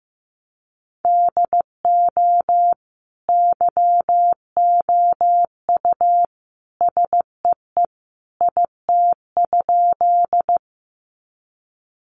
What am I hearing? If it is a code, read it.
Morse code, heard as a keyed tone, about 15 wpm, 700 Hz. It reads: DO YOU SEE IT?
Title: Sentences from Top 100 Words 15wpm